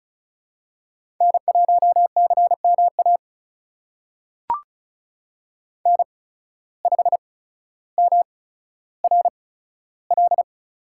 Morse code, heard as a keyed tone, about 36 wpm, spaced as for 15 wpm, 700 Hz. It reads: N1CMA N5MRL
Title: USA Call Signs - 5 Characters - 8x Character Spacing - 35wpm